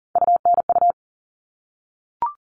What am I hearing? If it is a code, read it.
Morse code, heard as a keyed tone, about 40 wpm, 700 Hz. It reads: UNU
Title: All Cut Numbers - 3-Digit 40wpm